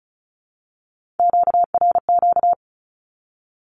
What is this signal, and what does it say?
Morse code, heard as a keyed tone, about 34 wpm, 700 Hz. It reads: QRQ
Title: Q-Codes Repeated 3x 35wpm